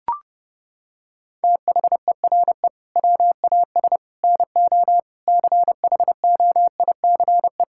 Morse code, heard as a keyed tone, about 28 wpm, 700 Hz. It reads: THERE WAS NO CHOICE
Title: Sentences from Top 1000 Words - Review Part 6 - 30wpm